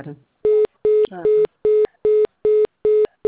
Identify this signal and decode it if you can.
Congestion tone (425 Hz), with speech.
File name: out- -2009- - - .6365